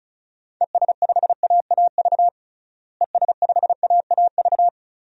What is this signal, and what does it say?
Morse code, heard as a keyed tone, about 34 wpm, 700 Hz. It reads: ES5AAV ES5AAV
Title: DXCC-100 Call Signs - 6 Characters - Repeated 3x - 35wpm